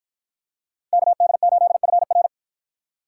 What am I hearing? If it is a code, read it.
Morse code, heard as a keyed tone, about 51 wpm, 700 Hz. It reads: KD8FR